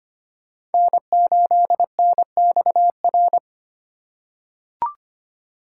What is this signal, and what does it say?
Morse code, heard as a keyed tone, about 25 wpm, 700 Hz. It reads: N8NXR
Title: USA Call Signs - 5 Characters - 3x Character Spacing - 25wpm